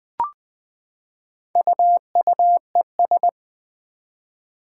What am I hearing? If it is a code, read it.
Morse code, heard as a keyed tone, about 20 wpm, 700 Hz. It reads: UUES
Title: Four Challenging Letter-Number Combinations 20wpm